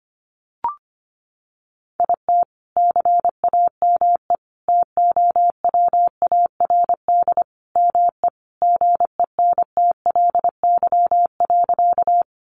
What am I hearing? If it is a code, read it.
Morse code, heard as a keyed tone, about 25 wpm, 700 Hz. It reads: IT CAME TOWARD ME GENTLY.
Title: Sentences from Top 1500 Words 25wpm